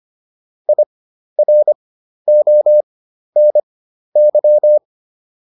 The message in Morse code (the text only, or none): IRONY